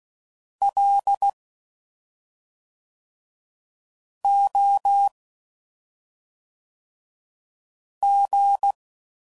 Morse code, heard as a keyed tone, about 16 wpm, 800 Hz. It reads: L O G
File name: Suffixes / Special ING's / SlowToFaster ING's